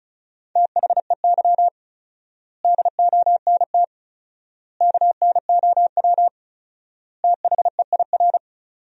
Morse code, heard as a keyed tone, about 34 wpm, 700 Hz. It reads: THEY DONT KNOW THEIR
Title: 4x Word-Spacing - Sentences from Top 300 Words - 35wpm